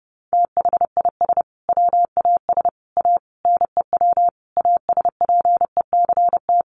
Morse code, heard as a keyed tone, about 28 wpm, 700 Hz. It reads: THIS WAS A NEW ASPECT
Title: Sentences from Top 1300 Words 30wpm